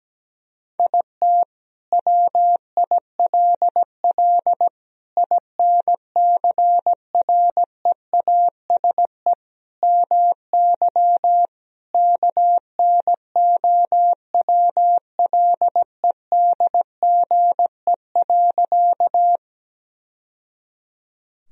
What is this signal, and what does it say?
Morse code, heard as a keyed tone, about 17 wpm, 700 Hz. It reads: IT WILL INCREASE MY KNOWLEDGE.
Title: Sentences from Top 1000 Words - Review Part 4 - 17wpm